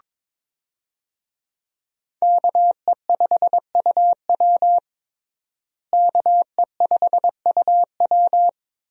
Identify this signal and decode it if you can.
Morse code, heard as a keyed tone, about 22 wpm, 700 Hz. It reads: KE5UW KE5UW
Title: USA Call Signs - 5 Characters - Repeated 2x - 22wpm